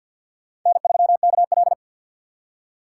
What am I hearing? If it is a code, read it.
Morse code, heard as a keyed tone, about 47 wpm, 700 Hz. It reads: N3KL
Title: USA Call Signs - 4 Characters - 6x Character Spacing - 50wpm